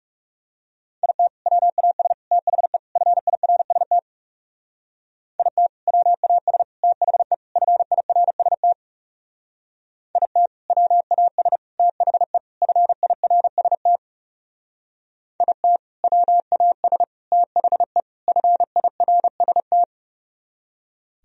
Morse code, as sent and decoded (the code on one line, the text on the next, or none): .. - / .-- .- ... / - .... / ..-. .. .-. ... - / .. - / .-- .- ... / - .... / ..-. .. .-. ... - / .. - / .-- .- ... / - .... / ..-. .. .-. ... - / .. - / .-- .- ... / - .... / ..-. .. .-. ... -
IT WAS THE FIRST IT WAS THE FIRST IT WAS THE FIRST IT WAS THE FIRST